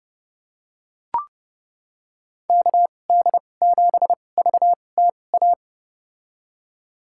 Morse code, heard as a keyed tone, about 30 wpm, spaced as for 22 wpm, 700 Hz. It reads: KD7VTA
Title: USA Call Signs - 6 Characters - 2x Character Spacing - 30wpm